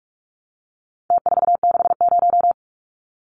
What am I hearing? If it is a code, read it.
Morse code, heard as a keyed tone, about 44 wpm, 700 Hz. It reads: T460
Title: Numbers - Contest Style - 4-Digit 45wpm